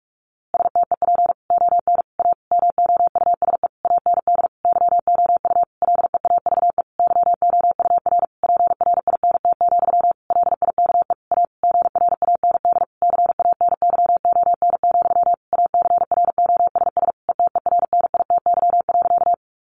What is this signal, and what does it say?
Morse code, heard as a keyed tone, about 44 wpm, 700 Hz. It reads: STEP ON A MOUSE AND YOU LEAVE YOUR PRINT, LIKE A GRAND CANYON, ACROSS ETERNITY.
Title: A Sound Of Thunder 45wpm